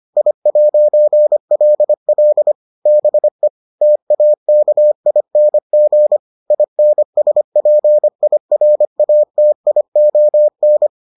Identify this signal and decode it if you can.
Morse code, heard as a keyed tone, about 25 wpm, 600 Hz. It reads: I'LL BE TAKING INSPIRATION